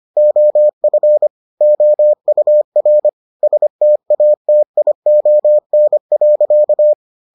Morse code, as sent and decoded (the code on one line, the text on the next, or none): --- ..-. / --- ..- .-. / ... - .- - .. --- -. .-.-.-
OF OUR STATION.